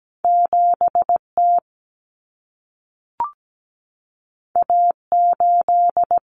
Morse code, heard as a keyed tone, about 17 wpm, 700 Hz. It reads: 7T A8
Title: Common Cut Numbers - 2-Digit 17wpm